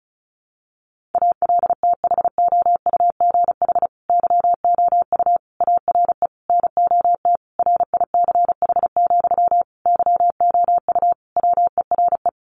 Morse code, heard as a keyed tone, about 34 wpm, 700 Hz. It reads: ALTHOUGH YOU ARE NOT RICH, YOU WERE
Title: Sentences from Top 1100 Words 35wpm